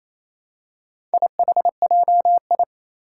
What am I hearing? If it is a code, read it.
Morse code, heard as a keyed tone, about 28 wpm, 700 Hz. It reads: IHJI
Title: Four Challenging Letter-Number Combinations 28wpm